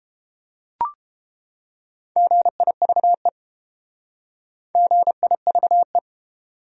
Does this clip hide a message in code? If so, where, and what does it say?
Yes: Morse code from 2.2 to 6.0 s, heard as a keyed tone, about 30 wpm, 700 Hz. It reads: GIVE GIVE